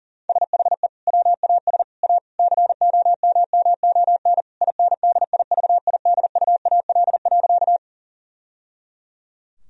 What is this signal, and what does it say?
Morse code, heard as a keyed tone, about 40 wpm, 700 Hz. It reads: SHE WAS A COMMON INDIVIDUAL.
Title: Sentences from Top 1000 Words - Review Part 4 - 40wpm